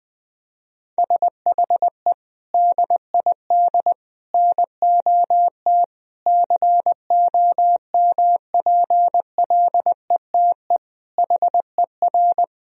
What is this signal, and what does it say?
Morse code, heard as a keyed tone, about 20 wpm, 700 Hz. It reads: SHE DID NOT COMPLETE HER
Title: Sentences from Top 1000 Words - Review Part 2 - 20wpm